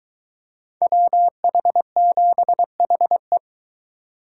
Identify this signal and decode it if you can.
Morse code, heard as a keyed tone, about 23 wpm, 700 Hz. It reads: WH7HE